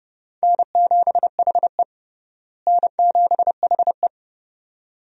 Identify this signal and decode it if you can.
Morse code, heard as a keyed tone, about 30 wpm, 700 Hz. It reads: N7HE N7HE
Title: USA Call Signs - 4 Characters - Repeated 2x - 30wpm